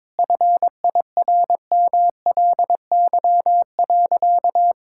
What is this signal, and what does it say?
Morse code, heard as a keyed tone, about 22 wpm, 690 Hz. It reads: FIRMLY.
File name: practice / 22wpm / Easy Sentences with Hints - 22wpm